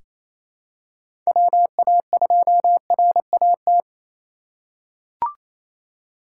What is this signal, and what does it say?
Morse code, heard as a keyed tone, about 28 wpm, 700 Hz. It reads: WA2RAT
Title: USA Call Signs - 6 Characters - 3x Character Spacing - 28wpm